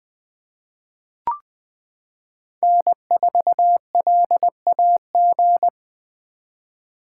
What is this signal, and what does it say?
Morse code, heard as a keyed tone, about 20 wpm, 700 Hz. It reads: N4LAG
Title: 2023 SOTA Chaser Call Signs - 20wpm